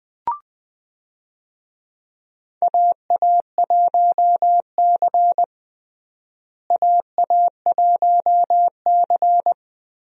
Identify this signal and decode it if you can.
Morse code, heard as a keyed tone, about 20 wpm, 700 Hz. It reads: AA1C AA1C
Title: USA Call Signs - 4 Characters - Repeated 2x - 20wpm